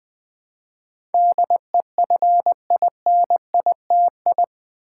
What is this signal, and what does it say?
Morse code, heard as a keyed tone, about 21 wpm, 700 Hz. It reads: DEFINITI
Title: Build A Word - 4 Letters At A Time - 10-Letter Words - 20wpm